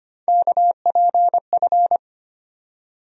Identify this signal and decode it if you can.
Morse code, heard as a keyed tone, about 25 wpm, 700 Hz. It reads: KPF
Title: USA Call Sign Postfixes - 3 Letters 25wpm